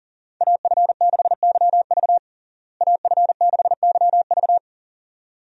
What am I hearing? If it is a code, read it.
Morse code, heard as a keyed tone, about 40 wpm, 700 Hz. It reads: AF6YV AF6YV
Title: USA Call Signs - 5 Characters - Repeated 2x - 40wpm